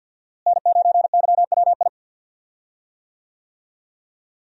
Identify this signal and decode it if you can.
Morse code, heard as a keyed tone, about 46 wpm, 700 Hz. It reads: N9YWI
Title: USA Call Signs - 5 Characters - Repeated 3x - 50wpm